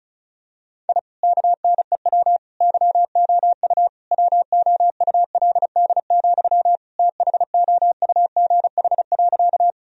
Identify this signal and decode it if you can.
Morse code, heard as a keyed tone, about 36 wpm, 700 Hz. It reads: I KNEW YOU WOULD, THOUGH.
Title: Sentences from Top 1000 Words - Review Part 2 - 35wpm